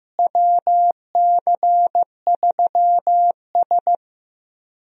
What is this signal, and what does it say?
Morse code, heard as a keyed tone, about 15 wpm, 700 Hz. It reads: WC3S
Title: USA Call Signs - 4 Characters - Repeated 3x - 15wpm